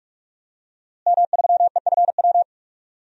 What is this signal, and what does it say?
Morse code, heard as a keyed tone, about 46 wpm, 700 Hz. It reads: M3EFW
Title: DXCC-100 Call Signs - 5 Characters - Repeated 2x - 45wpm